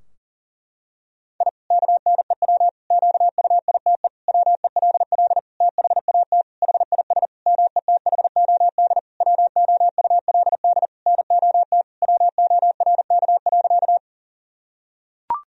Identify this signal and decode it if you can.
Morse code, heard as a keyed tone, about 40 wpm, 700 Hz. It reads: I KNEW QUITE WELL THAT HIS METHOD WOULD NOT WORK.
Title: Sentences from Top 1000 Words - Review Part 3 - 40wpm